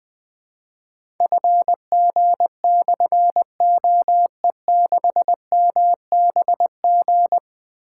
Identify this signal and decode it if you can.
Morse code, heard as a keyed tone, about 20 wpm, 700 Hz. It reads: FG/OE6MBG
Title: Difficult Call Signs 20wpm